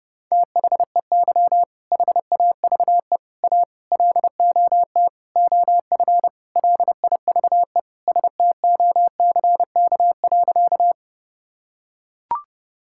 Morse code, heard as a keyed tone, about 32 wpm, 700 Hz. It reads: THEY HAVE A LOT OF LIVE STOCK.
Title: Sentences from Top 1000 Words - Review Part 5 - 30wpm